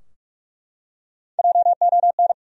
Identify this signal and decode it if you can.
Morse code, heard as a keyed tone, about 43 wpm, 700 Hz. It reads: JON
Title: USA Common Names - 45wpm